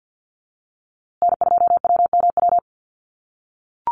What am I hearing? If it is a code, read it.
Morse code, heard as a keyed tone, about 50 wpm, 700 Hz. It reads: N2WMW
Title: USA Call Signs Repeated 2x 50wpm